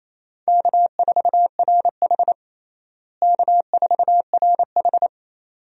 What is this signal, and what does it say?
Morse code, heard as a keyed tone, about 28 wpm, 700 Hz. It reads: K4RH K4RH
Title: USA Call Signs - 4 Characters - Repeated 2x - 28wpm